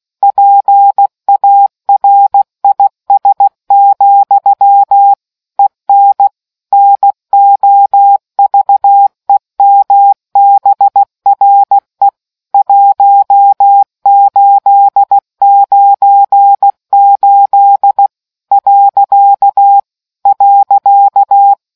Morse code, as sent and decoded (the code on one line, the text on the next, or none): .--. .- .-. .. ... --..-- / . -. / -. --- ...- . -- -... .-. / .---- ---.. ----. ---.. / .-.-.- / .-.-.-
PARIS, EN NOVEMBRE 1898 .